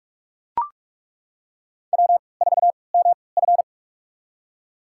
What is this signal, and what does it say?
Morse code, heard as a keyed tone, about 41 wpm, spaced as for 25 wpm, 700 Hz. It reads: W4MF
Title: USA Call Signs - 4 Characters - 3x Character Spacing - 45wpm